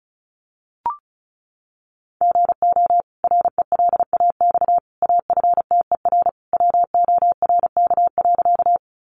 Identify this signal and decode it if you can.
Morse code, heard as a keyed tone, about 34 wpm, 700 Hz. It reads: GO RELAX AFTER WORK.
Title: Sentences from Top 1400 Words 35wpm